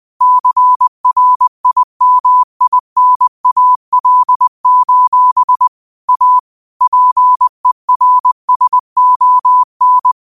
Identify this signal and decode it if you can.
Morse code, heard as a keyed tone, about 20 wpm, 1000 Hz. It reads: CRIMINAL: A PERSON
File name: qotdfiles / 20wpm